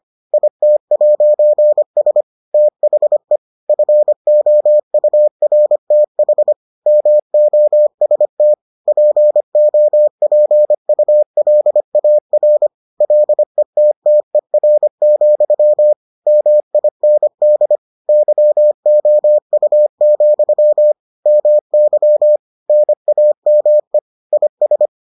Morse code, heard as a keyed tone, about 25 wpm, 600 Hz. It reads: IT'S THE FOURTH MOST POPULAR LETTER, MIND YOU, MY NAME IS